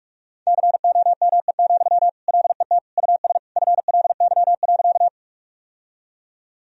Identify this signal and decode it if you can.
Morse code, heard as a keyed tone, about 43 wpm, 700 Hz. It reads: COME, LET US FLY.
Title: Sentences from Top 1000 Words - Review Part 2 - 45wpm